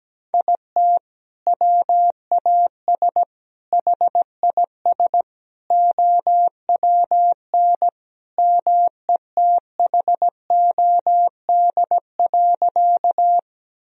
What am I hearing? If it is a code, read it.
Morse code, heard as a keyed tone, about 17 wpm, 700 Hz. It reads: IT WAS HIS OWN METHOD.
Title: Sentences from Top 1000 Words - Review Part 3 - 17wpm